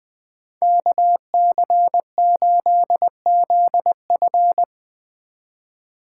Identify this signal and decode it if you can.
Morse code, heard as a keyed tone, about 23 wpm, 700 Hz. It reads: KC8ZF
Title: Speed Racing - USA Call Signs 20wpm